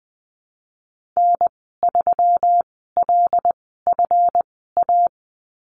Morse code, heard as a keyed tone, about 20 wpm, spaced as for 15 wpm, 700 Hz. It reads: N3LFA
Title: 161 Lesson 26g - Letter g - Call Signs 20wpm